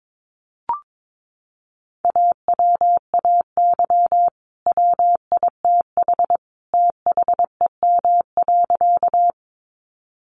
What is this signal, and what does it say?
Morse code, heard as a keyed tone, about 22 wpm, 700 Hz. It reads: AWAY WITH THEM.
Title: Sentences from Top 600 Words 22wpm